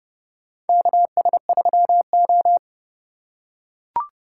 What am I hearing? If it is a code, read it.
Morse code, heard as a keyed tone, about 31 wpm, 700 Hz. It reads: KS3O
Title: USA Call Signs - 4x Character Spacing - 30wpm